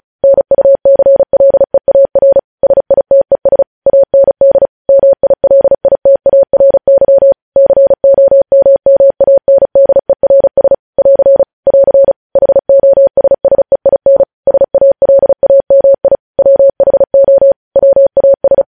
Morse code, heard as a keyed tone, about 36 wpm, 550 Hz. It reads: NUCLEAR SITES AND MILITARY COMMANDERS <AR> <AR> HOSSEIN SALAMI WHO WAS
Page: news headlines in morse code at 35 wpm